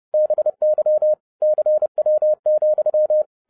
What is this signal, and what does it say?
Morse code, heard as a keyed tone, about 30 wpm, 610 Hz. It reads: BY CW,
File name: 2012-01-mp3 / n11-I DIDNT MISS THE SOLAR CYCLE 24-30wpm